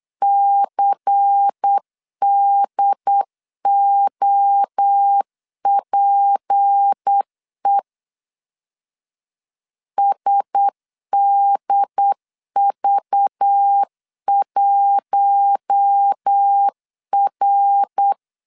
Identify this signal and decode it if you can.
Morse code, heard as a keyed tone, about 8 wpm, 800 Hz. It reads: CDOPE SDV1R
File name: LEZIONI / 6 PKX0 / 6 P 8wpm 3025